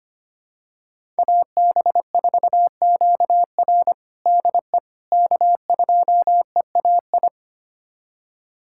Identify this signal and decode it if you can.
Morse code, heard as a keyed tone, about 25 wpm, 700 Hz. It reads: AB4QR DE K2EAI
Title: USA Callsign DE Callsign - 5 Characters - 25wpm